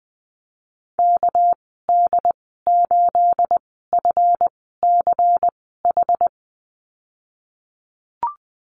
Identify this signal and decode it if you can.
Morse code, heard as a keyed tone, about 20 wpm, spaced as for 14 wpm, 700 Hz. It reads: KD8FCH
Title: 213 Lesson 34g - Number 8 - Call Signs 20wpm